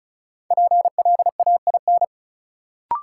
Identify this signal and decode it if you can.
Morse code, heard as a keyed tone, about 36 wpm, 700 Hz. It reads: PLAIN
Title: Sentences from Top 400 Words - New Word Drill - 35wpm